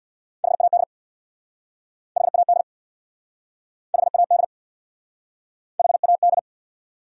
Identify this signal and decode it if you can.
Morse code, heard as a keyed tone, about 68 wpm, 700 Hz. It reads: HAD HAD HAD HAD